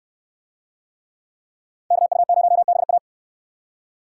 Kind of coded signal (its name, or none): Morse code